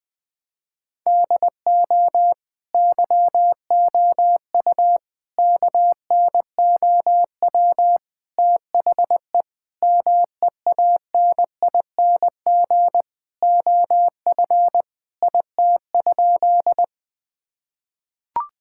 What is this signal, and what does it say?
Morse code, heard as a keyed tone, about 20 wpm, 700 Hz. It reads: DO YOU KNOW THE MEANING OF IT?
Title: Sentences from Top 1000 Words - Review Part 6 - 20wpm